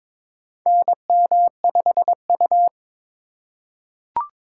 Morse code, heard as a keyed tone, about 22 wpm, 700 Hz. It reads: NM5U
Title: USA Call Signs - 4 Characters - 4x Character Spacing - 22wpm